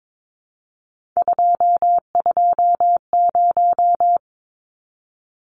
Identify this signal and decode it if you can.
Morse code, heard as a keyed tone, about 22 wpm, 700 Hz. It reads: 220
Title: Numbers - 3-Digit 22wpm